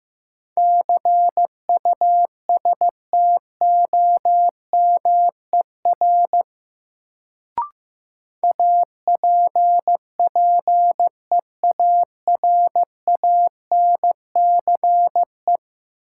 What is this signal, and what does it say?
Morse code, heard as a keyed tone, about 15 wpm, 700 Hz. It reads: CUSTOMER APPEARANCE
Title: Sentences from Top 1100 Words - New Word Drill - 15wpm